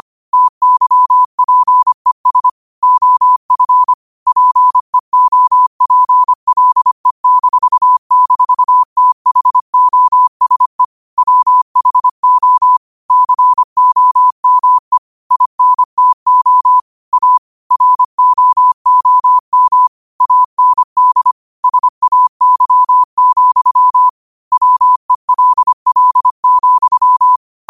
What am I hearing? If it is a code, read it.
Morse code, heard as a keyed tone, about 25 wpm, 1000 Hz. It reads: TYPES OF PEOPLE--THOSE WHO COME INTO A ROOM AND SAY, WELL,
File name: 25wpm